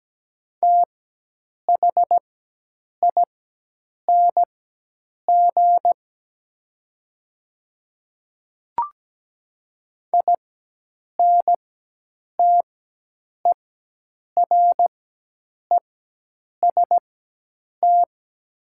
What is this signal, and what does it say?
Morse code, heard as a keyed tone, about 17 wpm, spaced as for 8 wpm, 700 Hz. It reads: THING INTEREST
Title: 4x Character Spacing - Top 500 Words - 17wpm